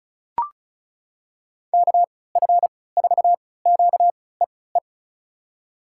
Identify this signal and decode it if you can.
Morse code, heard as a keyed tone, about 36 wpm, spaced as for 20 wpm, 700 Hz. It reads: KF4QEE
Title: USA Call Signs - 6 Characters - 3x Character Spacing - 35wpm